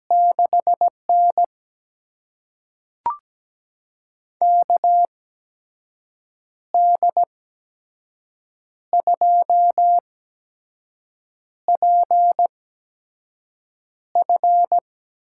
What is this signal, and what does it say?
Morse code, heard as a keyed tone, about 17 wpm, spaced as for 5 wpm, 700 Hz. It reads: 6N KD2PF
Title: USA Call Signs - 8x Character Spacing - 17wpm